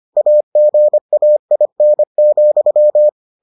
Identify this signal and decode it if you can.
Morse code, heard as a keyed tone, about 25 wpm, 600 Hz. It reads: AGAIN,